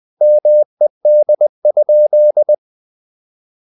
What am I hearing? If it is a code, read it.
Morse code, heard as a keyed tone, about 20 wpm, 600 Hz. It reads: MED?
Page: Ferie 20 wpm